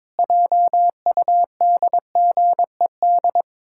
Morse code, heard as a keyed tone, about 22 wpm, 700 Hz. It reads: JUDGED
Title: Pangram Challenge v2 - 22wpm